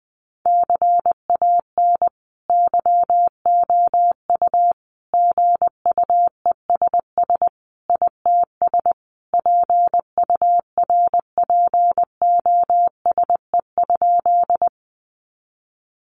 Morse code, heard as a keyed tone, about 20 wpm, 700 Hz. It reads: CAN YOU GUESS ITS PURPOSE?
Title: Sentences from Top 1500 Words 20wpm